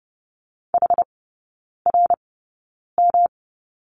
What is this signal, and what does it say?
Morse code, heard as a keyed tone, about 30 wpm, spaced as for 9 wpm, 700 Hz. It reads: HRM
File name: Lesson 18e - Letter m - Three Characters 30wpm